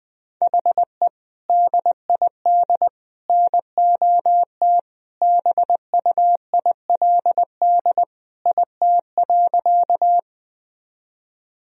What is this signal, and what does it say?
Morse code, heard as a keyed tone, about 20 wpm, 700 Hz. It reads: HE DID NOT BUILD IT.
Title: Sentences from Top 1000 Words - Review Part 1 - 20wpm